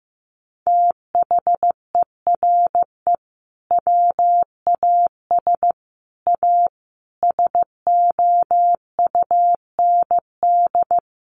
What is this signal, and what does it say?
Morse code, heard as a keyed tone, about 15 wpm, 700 Hz. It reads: THERE WAS A SOUND
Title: A Sound Of Thunder 15wpm